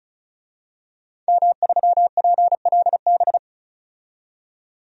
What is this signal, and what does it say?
Morse code, heard as a keyed tone, about 36 wpm, 700 Hz. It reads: M3PLB
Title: DXCC-100 Call Signs - 5 Characters - 35wpm